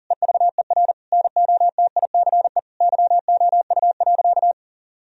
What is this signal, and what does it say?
Morse code, heard as a keyed tone, about 37 wpm, 700 Hz. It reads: EVER NOTICE YOU.
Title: Sentences from Top 1000 Words - Review Part 4 - 40wpm